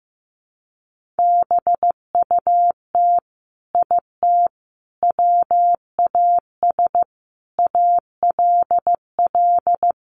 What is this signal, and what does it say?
Morse code, heard as a keyed tone, about 15 wpm, 700 Hz. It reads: BUT IT WAS ALL
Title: Sentences from Top 1500 Words 15wpm